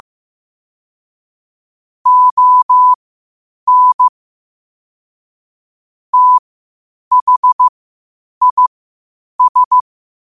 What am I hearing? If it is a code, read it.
Morse code, heard as a keyed tone, about 15 wpm, spaced as for 8 wpm, 1000 Hz. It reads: ON THIS
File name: qotdfiles / 8wpm